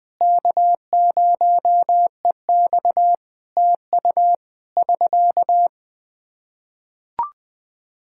Morse code, heard as a keyed tone, about 20 wpm, 700 Hz. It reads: K0EX TU <SK>
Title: K1USN SST QSO Practice 20wpm